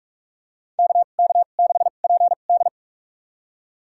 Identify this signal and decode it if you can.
Morse code, heard as a keyed tone, about 43 wpm, spaced as for 32 wpm, 700 Hz. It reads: KK6PD
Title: USA Call Signs - 5 Characters - 2x Character Spacing - 45wpm